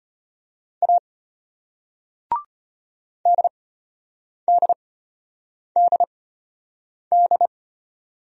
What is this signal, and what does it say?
Morse code, heard as a keyed tone, about 33 wpm, spaced as for 8 wpm, 700 Hz. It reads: A DDDD